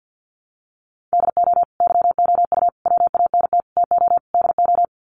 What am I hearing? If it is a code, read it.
Morse code, heard as a keyed tone, about 50 wpm, 700 Hz. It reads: DO YOU WANT TO DO
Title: Sets of 5 Words Encore 50wpm